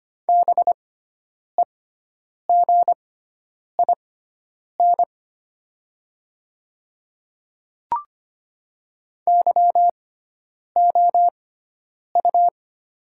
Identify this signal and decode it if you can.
Morse code, heard as a keyed tone, about 25 wpm, spaced as for 9 wpm, 700 Hz. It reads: BEGIN YOU
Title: 6x Character Spacing - Top 300 Words - 25wpm